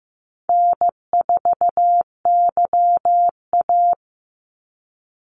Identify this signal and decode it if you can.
Morse code, heard as a keyed tone, about 15 wpm, 700 Hz. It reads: N4YA
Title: Call Signs Repeated 3x - 15wpm